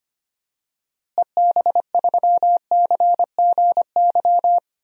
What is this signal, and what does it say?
Morse code, heard as a keyed tone, about 25 wpm, 700 Hz. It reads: EB3CGY
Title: DXCC-100 Call Signs - 6 Characters - Repeated 3x - 25wpm